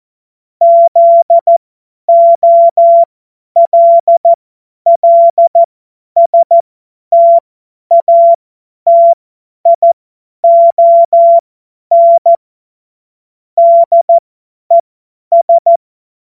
Morse code, heard as a keyed tone, about 14 wpm, spaced as for 10 wpm, 680 Hz. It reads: ZOLLSTATION DES
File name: Morsekurs / Witze / Witz 28. März 2023 50 F70